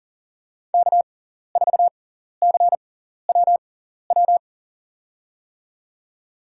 Morse code, heard as a keyed tone, about 40 wpm, spaced as for 14 wpm, 700 Hz. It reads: K4CWW
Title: USA Call Signs - 5 Characters - 6x Character Spacing - 40wpm